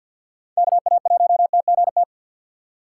Morse code, heard as a keyed tone, about 48 wpm, 700 Hz. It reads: KA1TGT
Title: USA Call Signs - 6 Characters - 4x Character Spacing - 50wpm